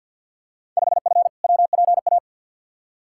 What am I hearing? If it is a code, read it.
Morse code, heard as a keyed tone, about 53 wpm, 700 Hz. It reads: 5F WPA
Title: Field Day Exchanges Encore - 50wpm